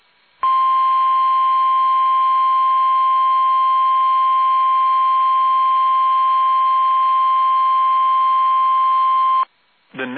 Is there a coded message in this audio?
Yes: Emergency Alert System (SAME) tones (the 1050 Hz attention tone) from 0.4 to 9.5 s.